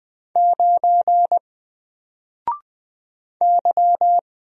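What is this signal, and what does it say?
Morse code, heard as a keyed tone, about 20 wpm, 700 Hz. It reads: 9 Y
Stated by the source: Single letter and number Morse drill at 20 wpm, from N4LCD.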